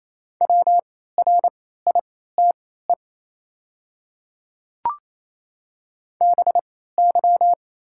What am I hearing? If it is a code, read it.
Morse code, heard as a keyed tone, about 28 wpm, spaced as for 16 wpm, 700 Hz. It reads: WRITE BY
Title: 3x Character Spacing - Top 500 Words - 28wpm